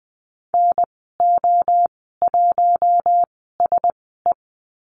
Morse code, heard as a keyed tone, about 20 wpm, spaced as for 14 wpm, 700 Hz. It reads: NO1SE
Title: 042 Lesson 8g - Number 1 - Call Signs 20wpm